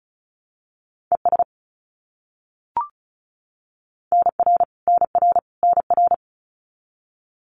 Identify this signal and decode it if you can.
Morse code, heard as a keyed tone, about 34 wpm, 700 Hz. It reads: ES NR NR NR
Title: Abbreviations Repeated 3x - 35wpm